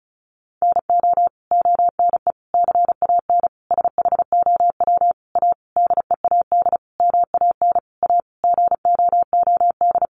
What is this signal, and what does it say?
Morse code, heard as a keyed tone, about 34 wpm, 700 Hz. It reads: NO ONE CAN SHOW A DEAD MAN A GOOD
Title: Sentences from Top 1500 Words 35wpm